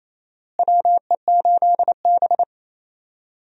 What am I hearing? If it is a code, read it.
Morse code, heard as a keyed tone, about 28 wpm, 700 Hz. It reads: WE8B
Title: USA Call Signs - 4 Characters - Repeated 2x - 28wpm